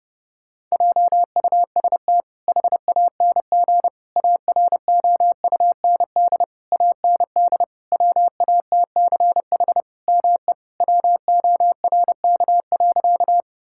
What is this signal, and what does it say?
Morse code, heard as a keyed tone, about 30 wpm, 700 Hz. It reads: JUST HANG AROUND AND WATCH ME WORK.
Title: Sentences from Top 1600 Words 30wpm